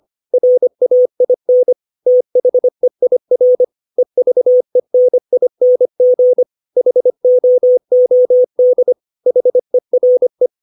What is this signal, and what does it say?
Morse code, heard as a keyed tone, about 25 wpm, 500 Hz. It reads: RAIN THEIR EVENING HOOD HERE
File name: 632 words / 632 Words 25WPM